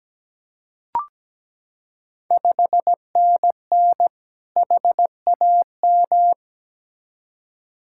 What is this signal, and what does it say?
Morse code, heard as a keyed tone, about 17 wpm, 700 Hz. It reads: 5NN HAM
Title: Ontario QSO Party Exchange 17wpm